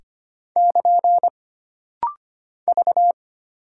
Morse code, heard as a keyed tone, about 25 wpm, 700 Hz. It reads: <KN> V
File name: Single Character (Letter:Number:Punctuation:Prosign) - Rapid-Fire - 25wpm